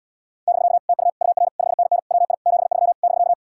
Morse code, heard as a keyed tone, about 80 wpm, 700 Hz. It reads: <BK> TU UR 5NN WI 73 <BK>